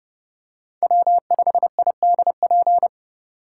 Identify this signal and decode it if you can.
Morse code, heard as a keyed tone, about 29 wpm, 700 Hz. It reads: W5IDP